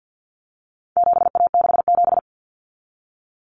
Morse code, heard as a keyed tone, about 50 wpm, 700 Hz. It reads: 7A67
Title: Common Cut Numbers - 4-Digit 50wpm